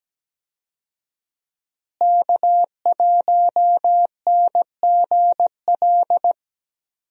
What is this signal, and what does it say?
Morse code, heard as a keyed tone, about 17 wpm, 700 Hz. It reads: K1NGL
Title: USA Call Signs - 5 Characters - Repeated 3x - 17wpm